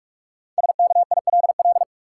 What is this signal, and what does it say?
Morse code, heard as a keyed tone, about 47 wpm, 700 Hz. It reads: SKILL
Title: Five Letter Words - 45wpm